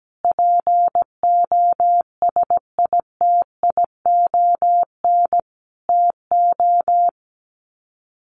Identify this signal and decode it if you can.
Morse code, heard as a keyed tone, about 17 wpm, 700 Hz. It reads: POSITION TO
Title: Sets of 5 Words Encore2 17wpm